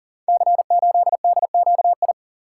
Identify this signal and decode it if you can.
Morse code, heard as a keyed tone, about 40 wpm, 700 Hz. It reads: C8DQI
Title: Code Groups - 5 Letter:Numbers - 40wpm